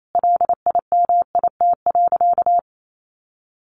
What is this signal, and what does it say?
Morse code, heard as a keyed tone, about 28 wpm, 700 Hz. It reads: LIMIT.
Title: Sentences from Top 1500 Words 28wpm